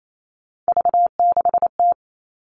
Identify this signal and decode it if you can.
Morse code, heard as a keyed tone, about 28 wpm, 700 Hz. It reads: V6T
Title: All Cut Numbers - 3-Digit 28wpm